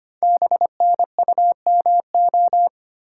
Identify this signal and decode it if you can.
Morse code, heard as a keyed tone, about 25 wpm, 700 Hz. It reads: BNUMO
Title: Code Groups - 5 Letters - 25wpm